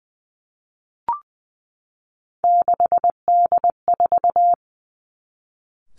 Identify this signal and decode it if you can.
Morse code, heard as a keyed tone, about 20 wpm, 700 Hz. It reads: 6D4
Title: Three Challenging Letter-Number Combinations 20wpm